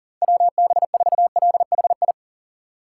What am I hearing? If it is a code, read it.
Morse code, heard as a keyed tone, about 40 wpm, 700 Hz. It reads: WB4LHI
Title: USA Call Signs - 6 Characters - 4x Character Spacing - 40wpm